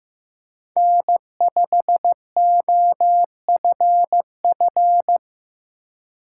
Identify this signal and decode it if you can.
Morse code, heard as a keyed tone, about 15 wpm, 700 Hz. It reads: N5OFF
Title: USA Call Signs - 2x Character Spacing - 15wpm